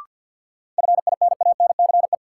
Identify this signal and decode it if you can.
Morse code, heard as a keyed tone, about 47 wpm, 700 Hz. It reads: FINANCE
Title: Sentences from Top 1500 Words - New Word Drill Alt - 50wpm